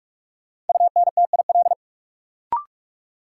Morse code, heard as a keyed tone, about 43 wpm, 700 Hz. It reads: UNTIL
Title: Sentences from Top 500 Words - New Word Drill - 45wpm